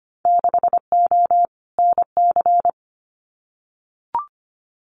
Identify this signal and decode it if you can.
Morse code, heard as a keyed tone, about 25 wpm, 700 Hz. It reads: 6O NC
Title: Winter Field Day Exchanges 25wpm